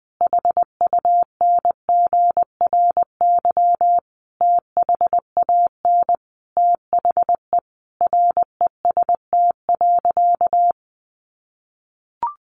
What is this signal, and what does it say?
Morse code, heard as a keyed tone, about 20 wpm, 700 Hz. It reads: HUNGRY THAN THE REST.
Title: Sentences from Top 1400 Words 20wpm